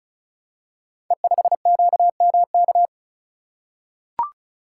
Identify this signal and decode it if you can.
Morse code, heard as a keyed tone, about 36 wpm, 690 Hz. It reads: E5QMK
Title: Code Groups - 5 Letter:Numbers - 35wpm